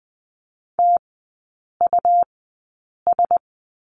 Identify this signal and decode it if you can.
Morse code, heard as a keyed tone, about 20 wpm, spaced as for 8 wpm, 700 Hz. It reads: TUS
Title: 107 Lesson 18e - Letter m - Three Characters 20wpm